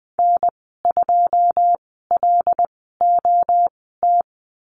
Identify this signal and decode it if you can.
Morse code, heard as a keyed tone, about 20 wpm, spaced as for 14 wpm, 700 Hz. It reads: N2LOT
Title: 088 Lesson 15g - Number 5 - Call Signs 20wpm